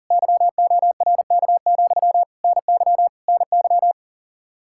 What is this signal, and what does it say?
Morse code, heard as a keyed tone, about 50 wpm, 700 Hz. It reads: YORK, NY NY